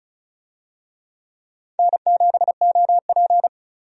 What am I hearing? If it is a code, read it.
Morse code, heard as a keyed tone, about 36 wpm, 700 Hz. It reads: N7OP
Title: USA Call Signs - 4 Characters - 35wpm